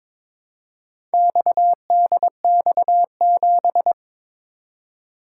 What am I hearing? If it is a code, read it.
Morse code, heard as a keyed tone, about 22 wpm, 700 Hz. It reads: XDX7
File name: practice / 22wpm / Four Letter-Number Combinations - 22wpm